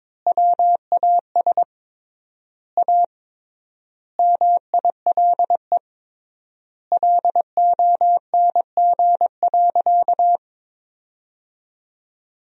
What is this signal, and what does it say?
Morse code, heard as a keyed tone, about 22 wpm, 700 Hz. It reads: WAS A MILE LONG.
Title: 3x Word-Spacing - Sentences from Top 500 Words - 22wpm